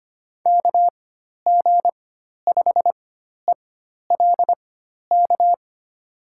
Morse code, heard as a keyed tone, about 25 wpm, spaced as for 12 wpm, 700 Hz. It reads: KG5ELK
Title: USA Call Signs - 4x Character Spacing - 25wpm